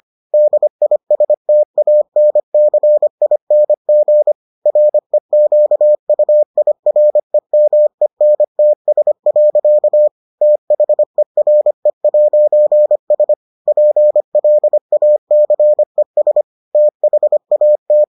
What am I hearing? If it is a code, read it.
Morse code, heard as a keyed tone, about 25 wpm, 600 Hz. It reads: DISTANCING REQUIREMENTS. THERE'S PLACES THAT